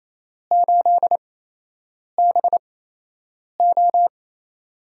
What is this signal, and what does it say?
Morse code, heard as a keyed tone, about 28 wpm, spaced as for 8 wpm, 700 Hz. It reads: 8BO
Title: USA Call Signs - 8x Character Spacing - 28wpm